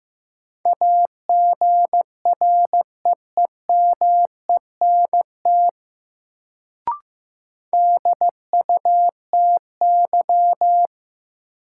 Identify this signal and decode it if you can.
Morse code, heard as a keyed tone, about 15 wpm, 700 Hz. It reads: AGREEMENT DUTY
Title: Sentences from Top 1500 Words - New Word Drill Alt - 15wpm